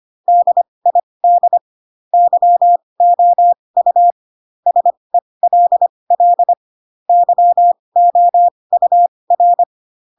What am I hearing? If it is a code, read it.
Morse code, heard as a keyed tone, about 25 wpm, spaced as for 20 wpm, 700 Hz. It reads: DID YOU SELL YOUR
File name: Pinocchio / Pinocchio ch8 15 25 ik1hge